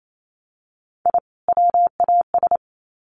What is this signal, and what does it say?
Morse code, heard as a keyed tone, about 28 wpm, 700 Hz. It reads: I WAS
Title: Sets of 2 Words 28wpm